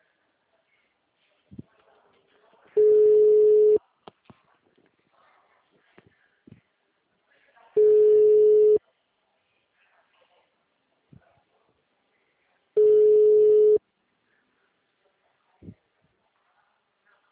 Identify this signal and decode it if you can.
Ringback tone (Europe, 425 Hz).